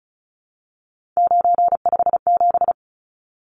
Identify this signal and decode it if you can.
Morse code, heard as a keyed tone, about 34 wpm, 700 Hz. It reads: 957
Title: Numbers - 3-Digit 35wpm